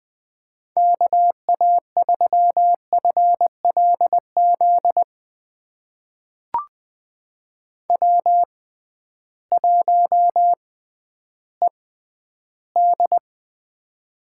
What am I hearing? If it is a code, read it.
Morse code, heard as a keyed tone, about 20 wpm, spaced as for 12 wpm, 700 Hz. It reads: KA3FLZ W1ED
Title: USA Call Signs - 6x Character Spacing - 20wpm